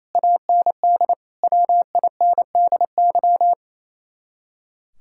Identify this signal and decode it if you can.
Morse code, heard as a keyed tone, about 36 wpm, 700 Hz. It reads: AND WINDY